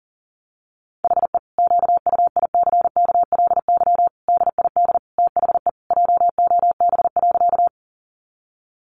Morse code, heard as a keyed tone, about 40 wpm, 700 Hz. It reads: HE QUICKLY DID THE JOB.
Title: Sentences from Top 700 Words 40wpm